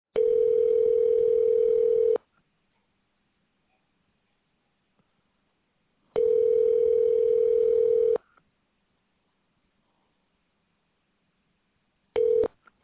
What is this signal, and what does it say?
Ringback tone (North America), 440 + 480 Hz.